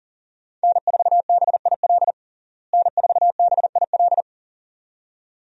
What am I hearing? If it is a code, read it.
Morse code, heard as a keyed tone, about 37 wpm, 700 Hz. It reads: N4BIL N4BIL
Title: USA Call Signs - 5 Characters - Repeated 2x - 40wpm